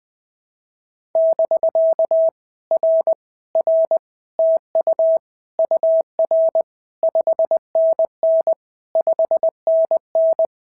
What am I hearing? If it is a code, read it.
Morse code, heard as a keyed tone, about 20 wpm, 650 Hz. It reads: <BK> R R TU UR 5NN 5NN
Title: POTA QSO Practice - 20wpm